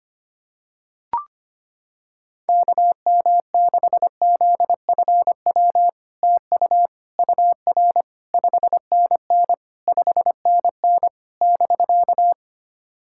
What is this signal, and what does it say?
Morse code, heard as a keyed tone, about 25 wpm, 700 Hz. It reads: KM6ZFW TU UR 5NN 5NN <BK>
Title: SOTA QSO Practice 25wpm